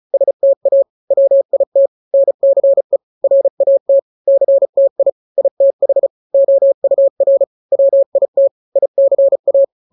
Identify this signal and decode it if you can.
Morse code, heard as a keyed tone, about 34 wpm, 550 Hz. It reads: STA WIT NCE RAT CTI ITH OUR WIT ICA
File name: 3-grams 35 30 WPM